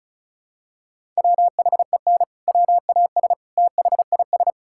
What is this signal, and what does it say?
Morse code, heard as a keyed tone, about 36 wpm, 700 Hz. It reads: WHEN WAS THIS
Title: Sentences from Top 1600 Words 35wpm